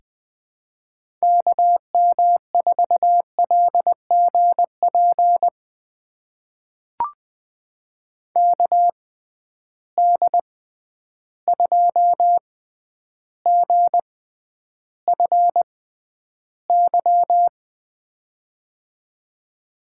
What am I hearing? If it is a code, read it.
Morse code, heard as a keyed tone, about 20 wpm, spaced as for 10 wpm, 700 Hz. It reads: KM4LGP KD2GFY
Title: USA Call Signs - 6 Characters - 6x Character Spacing - 20wpm